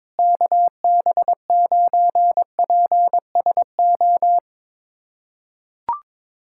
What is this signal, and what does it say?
Morse code, heard as a keyed tone, about 21 wpm, 700 Hz. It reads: KB9PSO
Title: USA Callsign DE Callsign 22wpm